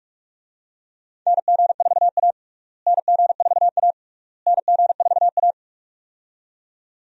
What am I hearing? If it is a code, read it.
Morse code, heard as a keyed tone, about 46 wpm, 700 Hz. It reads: NG4A NG4A NG4A
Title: USA Call Signs - 4 Characters - Repeated 3x - 45wpm